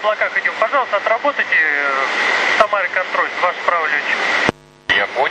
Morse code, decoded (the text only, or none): none